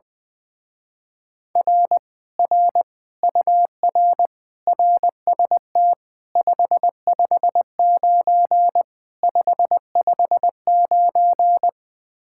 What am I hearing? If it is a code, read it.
Morse code, heard as a keyed tone, about 20 wpm, 700 Hz. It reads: R R UR RST 559 559
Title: Speed-Racing Common QSO Phrases - 20wpm